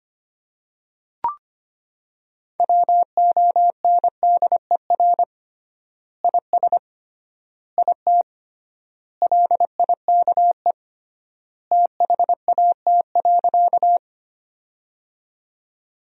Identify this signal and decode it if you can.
Morse code, heard as a keyed tone, about 25 wpm, 700 Hz. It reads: WONDER IS IT LIKE THAT.
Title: 3x Word-Spacing - Sentences from Top 500 Words - 25wpm